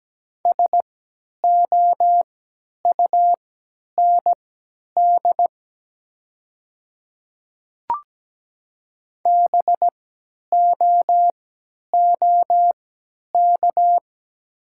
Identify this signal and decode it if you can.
Morse code, heard as a keyed tone, about 17 wpm, spaced as for 10 wpm, 700 Hz. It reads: SOUND BOOK
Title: 3x Character Spacing - Top 500 Words - 17wpm